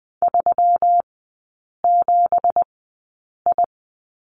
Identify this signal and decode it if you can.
Morse code, heard as a keyed tone, about 20 wpm, spaced as for 8 wpm, 700 Hz. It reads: 37I
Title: 166 Lesson 27e - Number 7 - Three Characters 20wpm